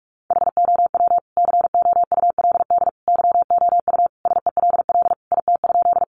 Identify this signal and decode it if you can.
Morse code, heard as a keyed tone, about 44 wpm, 700 Hz. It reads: HOW COULD YOU SELL IT?
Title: Sentences from Top 1100 Words 45wpm